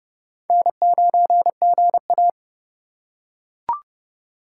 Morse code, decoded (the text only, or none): N9GA